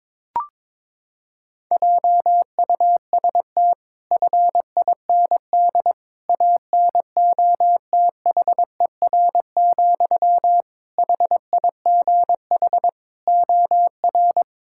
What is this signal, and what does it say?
Morse code, heard as a keyed tone, about 21 wpm, 700 Hz. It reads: JUST FIND ANOTHER, HIGH OR
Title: Sentences from Top 1600 Words 22wpm